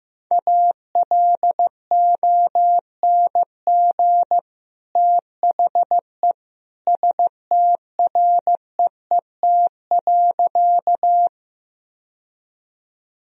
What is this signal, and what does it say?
Morse code, heard as a keyed tone, about 16 wpm, 700 Hz. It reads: ALONG THE STREET.
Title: Sentences from Top 1000 Words - Review Part 3 - 15wpm